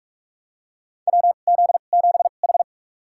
Morse code, heard as a keyed tone, about 46 wpm, spaced as for 33 wpm, 700 Hz. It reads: WZ7H
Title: USA Call Signs - 2x Character Spacing - 45wpm